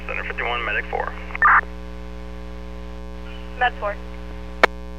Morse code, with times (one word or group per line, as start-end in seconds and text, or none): none